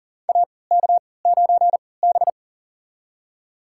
Morse code, heard as a keyed tone, about 40 wpm, spaced as for 23 wpm, 700 Hz. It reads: AK9B
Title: USA Call Signs - 4 Characters - 3x Character Spacing - 40wpm